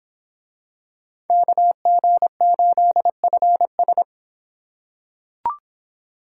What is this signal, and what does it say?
Morse code, heard as a keyed tone, about 26 wpm, 700 Hz. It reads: KG8FS